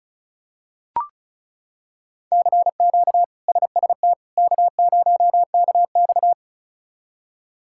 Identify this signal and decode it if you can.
Morse code, heard as a keyed tone, about 36 wpm, 700 Hz. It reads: CQ SST K0KX
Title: K1USN SST QSO Practice - 35wpm